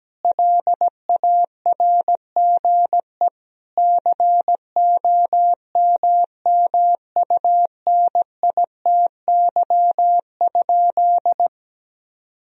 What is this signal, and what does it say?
Morse code, heard as a keyed tone, about 17 wpm, 700 Hz. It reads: LARGE COMMUNITY?
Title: Sentences from Top 1000 Words - Review Part 4 - 17wpm